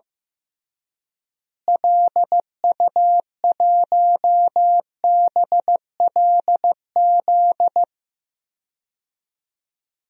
Morse code, heard as a keyed tone, about 15 wpm, 700 Hz. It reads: LU1BLZ
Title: DXCC-100 Call Signs - 6 Characters - Repeated 3x - 15wpm